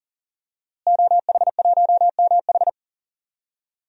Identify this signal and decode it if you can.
Morse code, heard as a keyed tone, about 39 wpm, 700 Hz. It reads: OH1MH
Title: DXCC-100 Call Signs - 5 Characters - Repeated 2x - 40wpm